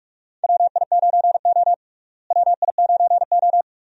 Morse code, heard as a keyed tone, about 41 wpm, 700 Hz. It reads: WI9O WI9O
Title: USA Call Signs - 4 Characters - Repeated 2x - 45wpm